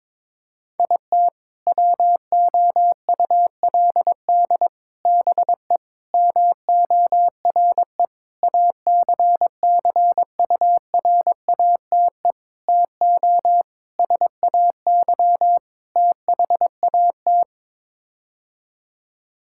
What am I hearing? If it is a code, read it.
Morse code, heard as a keyed tone, about 22 wpm, 700 Hz. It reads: IT WOULD BE MORE ACCURATE TO SAY THAT
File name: practice / 22wpm / Sets of 8 Words - 22wpm